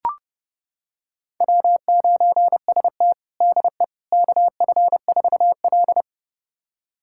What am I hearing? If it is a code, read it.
Morse code, heard as a keyed tone, about 30 wpm, 700 Hz. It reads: W9ST DE KF4L
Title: USA Callsign DE Callsign - 4 Characters - 30wpm